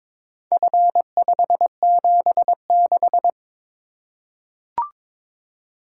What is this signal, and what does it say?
Morse code, heard as a keyed tone, about 22 wpm, 700 Hz. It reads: F576
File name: practice / 22wpm / Four Letter-Number Combinations - 22wpm